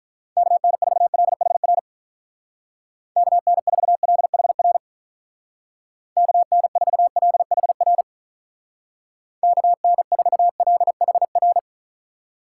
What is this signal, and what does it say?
Morse code, heard as a keyed tone, about 44 wpm, 700 Hz. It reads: KN4LHR KN4LHR KN4LHR KN4LHR